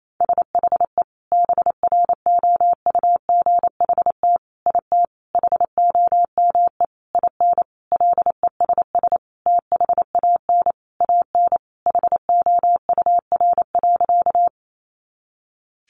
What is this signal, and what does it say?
Morse code, heard as a keyed tone, about 28 wpm, 700 Hz. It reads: SHE BROUGHT IT HOME IN LESS THAN AN HOUR.
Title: Sentences from Top 500 Words 28wpm